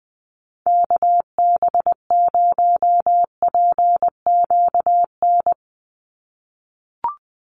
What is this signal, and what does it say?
Morse code, heard as a keyed tone, about 20 wpm, 700 Hz. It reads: KB0PQN
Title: USA Call Signs 20wpm